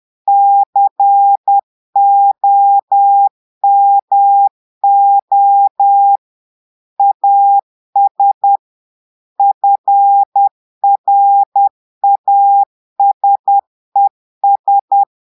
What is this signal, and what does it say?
Morse code, heard as a keyed tone, about 10 wpm, 800 Hz. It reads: COMO AS FRASES
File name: portugais texte en clair 2 vitesse 10 mots minutes